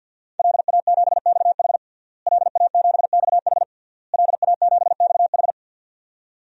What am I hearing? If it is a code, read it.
Morse code, heard as a keyed tone, about 51 wpm, 700 Hz. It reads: LA7XH LA7XH LA7XH
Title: DXCC-100 Call Signs - 5 Characters - Repeated 3x - 50wpm